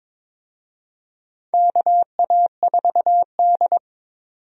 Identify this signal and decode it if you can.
Morse code, heard as a keyed tone, about 21 wpm, 700 Hz. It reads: KA4D
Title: USA Call Signs - 4 Characters - 22wpm